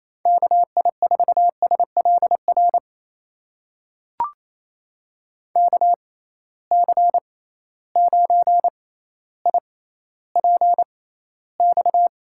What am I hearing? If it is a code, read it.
Morse code, heard as a keyed tone, about 27 wpm, spaced as for 14 wpm, 700 Hz. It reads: KI4SLR KC9IPX
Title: USA Call Signs - 6 Characters - 6x Character Spacing - 28wpm